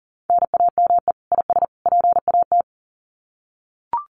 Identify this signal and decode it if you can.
Morse code, heard as a keyed tone, about 40 wpm, 700 Hz. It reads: NAME IS PAT
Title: Common QSO Phrases 40wpm